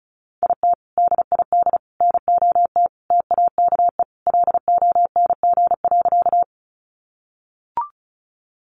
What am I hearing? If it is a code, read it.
Morse code, heard as a keyed tone, about 34 wpm, 700 Hz. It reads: IT DID NOT TAKE LONG.
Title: Sentences from Top 600 Words 35wpm